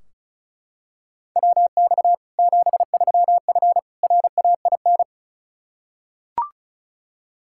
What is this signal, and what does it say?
Morse code, heard as a keyed tone, about 34 wpm, 700 Hz. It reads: WX 73F RAIN
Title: Practice QSO - Exchange 2 - Info - 35wpm